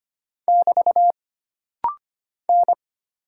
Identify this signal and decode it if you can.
Morse code, heard as a keyed tone, about 25 wpm, 700 Hz. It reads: <BT> N
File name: Single Character (Letter:Number:Punctuation:Prosign) - Rapid-Fire - 25wpm